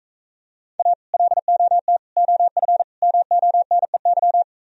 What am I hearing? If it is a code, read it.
Morse code, heard as a keyed tone, about 44 wpm, 700 Hz. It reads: A LOT OF MONEY